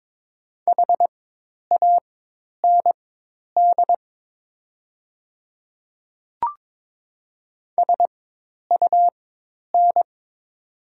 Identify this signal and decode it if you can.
Morse code, heard as a keyed tone, about 22 wpm, spaced as for 10 wpm, 700 Hz. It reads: HAND SUN
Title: 4x Character Spacing - Top 300 Words - 22wpm